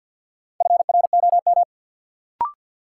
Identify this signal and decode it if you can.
Morse code, heard as a keyed tone, about 54 wpm, 700 Hz. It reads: FROM
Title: Sentences from Top 100 Words - New Word Drill - 50wpm